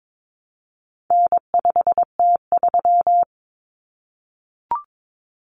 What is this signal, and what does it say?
Morse code, heard as a keyed tone, about 22 wpm, 700 Hz. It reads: N5T3
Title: Common Cut Numbers - 4-Digit 22wpm